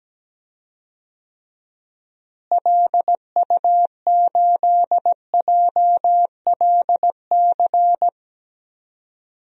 Morse code, heard as a keyed tone, about 17 wpm, 700 Hz. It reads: LU8JLC
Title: DXCC-100 Call Signs - 6 Characters - Repeated 2x - 17wpm